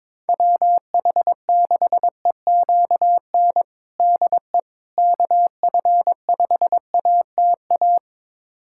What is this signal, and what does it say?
Morse code, heard as a keyed tone, about 22 wpm, 700 Hz. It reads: WH6EQN DE KF5ATA
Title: USA Callsign DE Callsign - 6 Characters - 22wpm